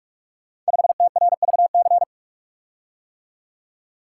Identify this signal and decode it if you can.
Morse code, heard as a keyed tone, about 44 wpm, 700 Hz. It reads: 5TRVC
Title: Code Groups - 5 Letter:Numbers - 45wpm